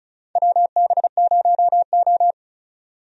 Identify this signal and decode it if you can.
Morse code, heard as a keyed tone, about 34 wpm, 700 Hz. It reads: WB0O
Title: USA Call Signs - 4 Characters - 35wpm